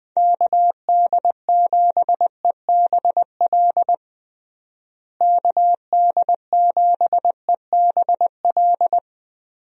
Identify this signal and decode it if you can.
Morse code, heard as a keyed tone, about 21 wpm, 700 Hz. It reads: KD7EBL KD7EBL
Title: USA Call Signs - 6 Characters - Repeated 3x - 20wpm